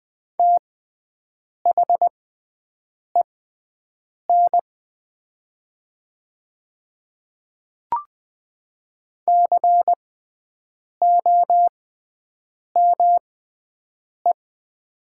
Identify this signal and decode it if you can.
Morse code, heard as a keyed tone, about 20 wpm, spaced as for 7 wpm, 700 Hz. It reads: THEN COME
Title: 6x Character Spacing - Top 100 Words - 20wpm